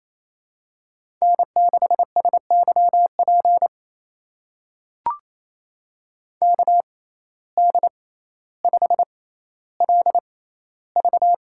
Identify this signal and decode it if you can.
Morse code, heard as a keyed tone, about 28 wpm, spaced as for 14 wpm, 700 Hz. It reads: N6SYP KD5LV
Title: USA Call Signs - 5 Characters - 6x Character Spacing - 28wpm